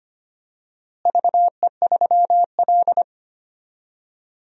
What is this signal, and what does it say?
Morse code, heard as a keyed tone, about 25 wpm, 700 Hz. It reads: VE3L
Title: DXCC-100 Call Signs - 4 Characters - Repeated 3x - 25wpm